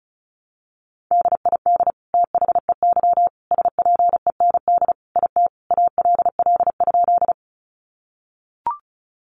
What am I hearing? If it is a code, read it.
Morse code, heard as a keyed tone, about 34 wpm, 700 Hz. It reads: DID THEY SPEND IT ALL?
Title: Sentences from Top 800 Words 35wpm